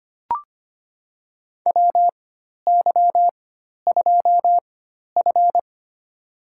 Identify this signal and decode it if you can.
Morse code, heard as a keyed tone, about 25 wpm, spaced as for 12 wpm, 700 Hz. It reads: WY2F
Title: USA Call Signs - 4x Character Spacing - 25wpm